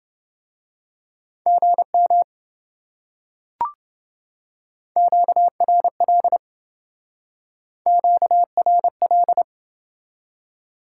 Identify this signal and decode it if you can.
Morse code, heard as a keyed tone, about 26 wpm, 700 Hz. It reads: GM QRL QRL